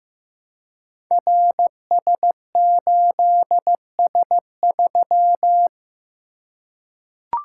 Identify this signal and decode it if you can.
Morse code, heard as a keyed tone, about 15 wpm, 700 Hz. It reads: RS8S3
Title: Code Groups - 5 Letter:Numbers - 15wpm